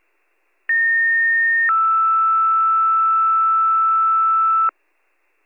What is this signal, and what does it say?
Two-tone page, heard as a one-second tone, then a three-second tone, about 1850 then 1320 Hz, with noise. It is heard from 0.7 to 4.7 s.